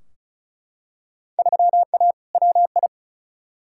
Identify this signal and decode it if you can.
Morse code, heard as a keyed tone, about 36 wpm, 700 Hz. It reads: 3A WI
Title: Field Day Exchanges - 35wpm